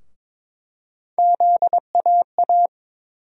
Morse code, heard as a keyed tone, about 22 wpm, 700 Hz. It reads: ZAA
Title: USA Call Sign Postfixes - 3 Letters 22wpm